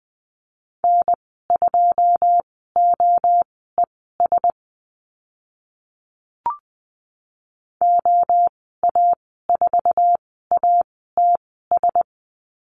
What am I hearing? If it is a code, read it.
Morse code, heard as a keyed tone, about 19 wpm, spaced as for 14 wpm, 700 Hz. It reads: N2OES OA4ATS
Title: 082 Lesson 14g - Number 2 - Call Signs 20wpm